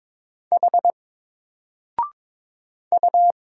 Morse code, heard as a keyed tone, about 22 wpm, 700 Hz. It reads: H U
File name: Single-Letter-Number-22wpm